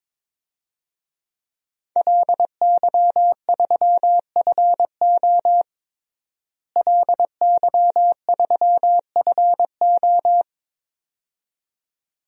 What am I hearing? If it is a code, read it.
Morse code, heard as a keyed tone, about 22 wpm, 700 Hz. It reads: LY3FO LY3FO
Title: DXCC-100 Call Signs - 5 Characters - Repeated 2x - 22wpm